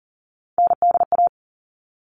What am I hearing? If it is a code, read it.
Morse code, heard as a keyed tone, about 40 wpm, 700 Hz. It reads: NDA
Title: All Cut Numbers - 3-Digit 40wpm